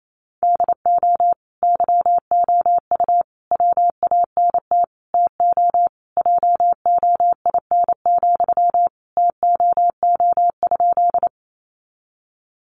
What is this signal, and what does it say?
Morse code, heard as a keyed tone, about 28 wpm, 700 Hz. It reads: DO YOU WANT TO JOIN, TOO?
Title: Sentences from Top 1400 Words 28wpm